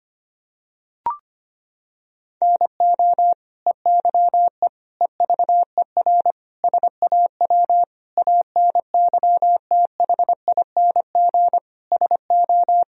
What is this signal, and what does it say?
Morse code, heard as a keyed tone, about 25 wpm, 700 Hz. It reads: NO EYE EVER SAW ANYTHING SO
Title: Sentences from Top 1000 Words - Review Part 1 - 25wpm